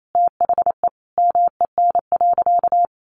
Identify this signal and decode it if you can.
Morse code, heard as a keyed tone, about 28 wpm, 700 Hz. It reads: THE MEN.
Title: A Sound Of Thunder 28wpm